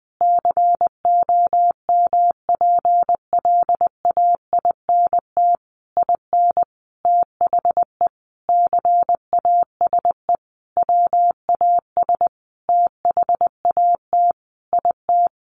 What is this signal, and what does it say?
Morse code, heard as a keyed tone, about 20 wpm, 700 Hz. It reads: COMPLAINT IN THE CASE WAS THAT IT
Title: Sentences from Top 1400 Words 20wpm